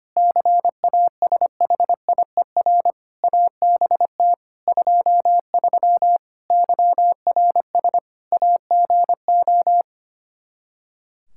Morse code, heard as a keyed tone, about 25 wpm, 700 Hz. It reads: CASHIER ABT 23 YRS AGO
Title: Practice QSO - Exchange 3 - More Info - 25wpm